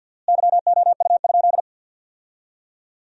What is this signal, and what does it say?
Morse code, heard as a keyed tone, about 53 wpm, 700 Hz. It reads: YOU?
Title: 4x Word-Spacing - Sentences from Top 300 Words - 50wpm